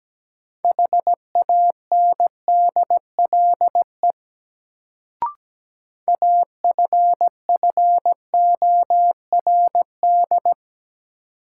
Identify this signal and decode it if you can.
Morse code, heard as a keyed tone, about 17 wpm, 700 Hz. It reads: HANDLE AFFORD
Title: Sentences from Top 1100 Words - New Word Drill - 17wpm